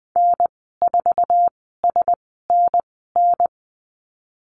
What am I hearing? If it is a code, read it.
Morse code, heard as a keyed tone, about 19 wpm, spaced as for 14 wpm, 700 Hz. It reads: N4SNN
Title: 048 Lesson 9g - Number 4 - Call Signs 20wpm